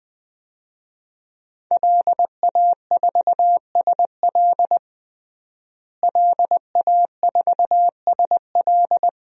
Morse code, heard as a keyed tone, about 21 wpm, 700 Hz. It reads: LA4SL LA4SL
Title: DXCC-100 Call Signs - 5 Characters - Repeated 3x - 20wpm